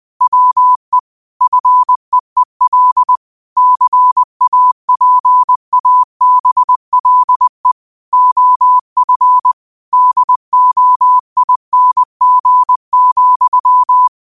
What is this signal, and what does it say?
Morse code, heard as a keyed tone, about 20 wpm, 1000 Hz. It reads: WE FEEL CAPABLE OF DOING,